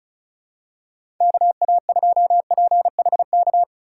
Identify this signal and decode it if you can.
Morse code, heard as a keyed tone, about 36 wpm, 700 Hz. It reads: KA2PHK
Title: USA Call Signs - 4x Character Spacing - 35wpm